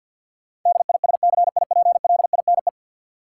Morse code, heard as a keyed tone, about 53 wpm, 700 Hz. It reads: DISCIPLINE
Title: Sentences from Top 900 Words - New Word Drill - 50wpm